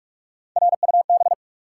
Morse code, heard as a keyed tone, about 44 wpm, 700 Hz. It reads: RUB
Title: Three Letter Words - 45wpm